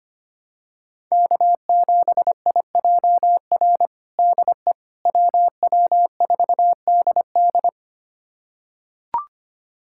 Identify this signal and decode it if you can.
Morse code, heard as a keyed tone, about 25 wpm, 700 Hz. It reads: K7IJR DE WW4DD
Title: USA Callsign DE Callsign - 5 Characters - 25wpm